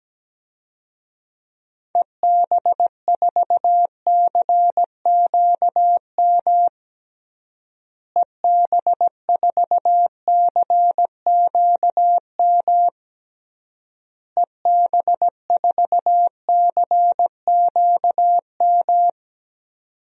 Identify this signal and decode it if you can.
Morse code, heard as a keyed tone, about 17 wpm, 700 Hz. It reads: EB4CQM EB4CQM EB4CQM
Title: DXCC-100 Call Signs - 6 Characters - Repeated 3x - 17wpm